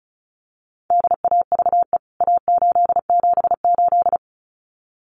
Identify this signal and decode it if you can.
Morse code, heard as a keyed tone, about 34 wpm, 700 Hz. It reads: DAVE A878
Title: CWT Exchange 35wpm